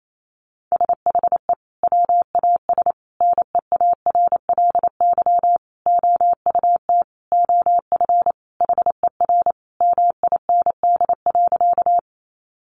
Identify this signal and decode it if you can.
Morse code, heard as a keyed tone, about 28 wpm, 700 Hz. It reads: SHE WAS NEARLY OUT OF HER MIND.
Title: Sentences from Top 1300 Words 28wpm